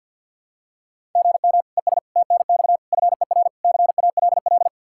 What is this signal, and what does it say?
Morse code, heard as a keyed tone, about 51 wpm, 700 Hz. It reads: GM ES TNX FER CALL
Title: Practice QSO - Exchange 1 - Beginning - 50wpm